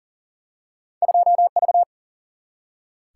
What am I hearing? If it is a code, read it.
Morse code, heard as a keyed tone, about 40 wpm, 700 Hz. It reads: 2V
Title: Single-Letter Single-Number 40wpm